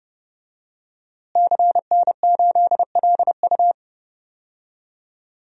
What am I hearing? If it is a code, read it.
Morse code, heard as a keyed tone, about 30 wpm, 700 Hz. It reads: CN8LU
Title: DXCC-100 Call Signs - 5 Characters - Repeated 2x - 30wpm